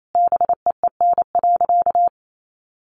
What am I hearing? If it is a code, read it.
Morse code, heard as a keyed tone, about 28 wpm, 700 Hz. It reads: BEEN.
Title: Sentences from Top 1000 Words 28wpm